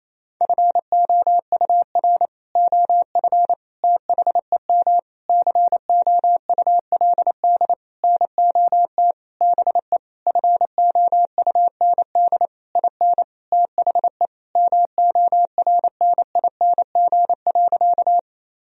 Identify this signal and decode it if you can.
Morse code, heard as a keyed tone, about 28 wpm, 700 Hz. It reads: FOUR OF THEM COULD NOT BE FOUND IN THE MORNING.
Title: Sentences from Top 1000 Words - Review Part 1 - 28wpm